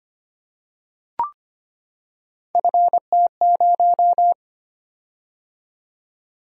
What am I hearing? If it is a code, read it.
Morse code, heard as a keyed tone, about 25 wpm, 700 Hz. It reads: FT0
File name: Three Letter-Number Combinations - 25wpm